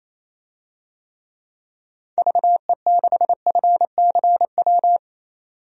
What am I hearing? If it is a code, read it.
Morse code, heard as a keyed tone, about 27 wpm, 700 Hz. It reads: VE6FCW
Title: DXCC-100 Call Signs - 6 Characters - 28wpm